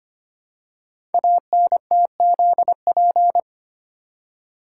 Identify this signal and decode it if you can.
Morse code, heard as a keyed tone, about 25 wpm, 700 Hz. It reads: ANTZP
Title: Code Groups - 5 Letters - 25wpm